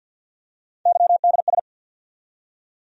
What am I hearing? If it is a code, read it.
Morse code, heard as a keyed tone, about 47 wpm, 700 Hz. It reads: YDS
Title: USA Call Sign Postfixes - 3 Letters 50wpm